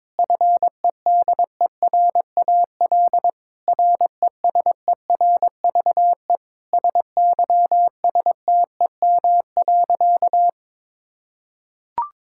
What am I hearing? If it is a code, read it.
Morse code, heard as a keyed tone, about 22 wpm, 700 Hz. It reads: FEDERAL RESERVE SYSTEM.
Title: Sentences from Top 1000 Words - Review Part 6 - 22wpm